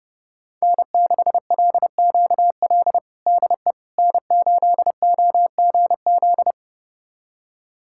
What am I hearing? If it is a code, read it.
Morse code, heard as a keyed tone, about 32 wpm, 700 Hz. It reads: N6LQL DE N8OGZ
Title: USA Callsign DE Callsign - 5 Characters - 30wpm